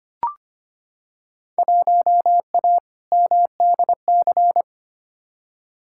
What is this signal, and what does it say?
Morse code, heard as a keyed tone, about 25 wpm, 700 Hz. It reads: 1A MDC
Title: Field Day Exchanges - 25wpm